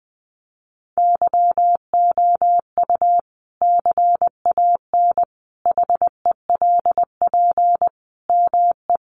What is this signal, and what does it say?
Morse code, heard as a keyed tone, about 20 wpm, 700 Hz. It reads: YOU CAN HELP ME
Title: Sentences from Top 200 Words 20wpm